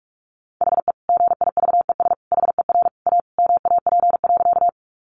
Morse code, heard as a keyed tone, about 44 wpm, 700 Hz. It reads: HE GIVES HER A MAP.
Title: Additional Easy Phrases 45wpm